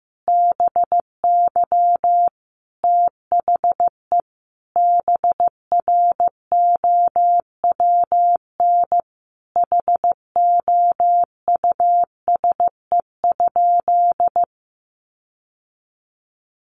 Morse code, heard as a keyed tone, about 15 wpm, 700 Hz. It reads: BY THE BROWN HOUSE?
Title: Sentences from Top 1200 Words 15wpm